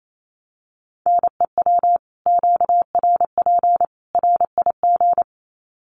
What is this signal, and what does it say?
Morse code, heard as a keyed tone, about 28 wpm, 700 Hz. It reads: NEW QRP RIG
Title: Common QSO Phrases 28wpm